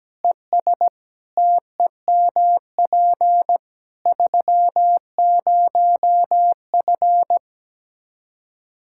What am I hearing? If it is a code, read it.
Morse code, heard as a keyed tone, about 17 wpm, 700 Hz. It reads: ES TEMP 30F
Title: K1USN SST QSO Practice 17wpm